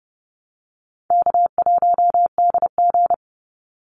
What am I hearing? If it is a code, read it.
Morse code, heard as a keyed tone, about 30 wpm, 700 Hz. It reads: K1DG
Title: Easy Call Signs Repeated 2x - 30wpm